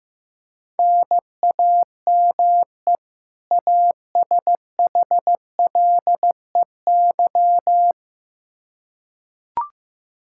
Morse code, heard as a keyed tone, about 15 wpm, 700 Hz. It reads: NAME ASHLEY
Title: Practice QSO - Exchange 1 - Beginning - 15wpm